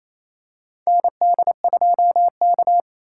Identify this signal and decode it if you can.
Morse code, heard as a keyed tone, about 28 wpm, 700 Hz. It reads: ND2K
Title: USA Call Signs - 2x Character Spacing - 28wpm